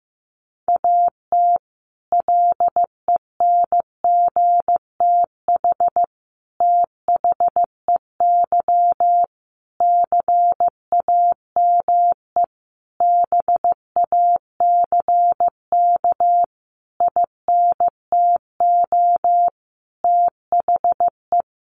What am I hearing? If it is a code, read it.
Morse code, heard as a keyed tone, about 15 wpm, 700 Hz. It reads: AT LENGTH THEY CAME BACK INTO THE
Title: Sentences from Top 1000 Words 15wpm